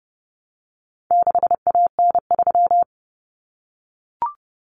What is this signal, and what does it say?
Morse code, heard as a keyed tone, about 30 wpm, 700 Hz. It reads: 6AN3
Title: Common Cut Numbers - 4-Digit 30wpm